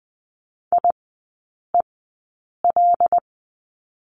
Morse code, heard as a keyed tone, about 20 wpm, spaced as for 9 wpm, 690 Hz. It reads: IEL
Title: 093 Lesson 16e - Letter c - Three Characters 20wpm